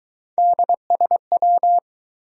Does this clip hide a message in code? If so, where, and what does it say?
Yes: Morse code from 0.4 to 1.8 s, heard as a keyed tone, about 23 wpm, 700 Hz. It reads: DSW